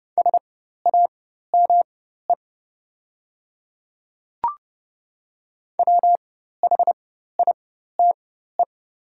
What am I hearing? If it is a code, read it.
Morse code, heard as a keyed tone, about 31 wpm, spaced as for 14 wpm, 690 Hz. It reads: SAME WHITE
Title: 4x Character Spacing - Top 300 Words - 30wpm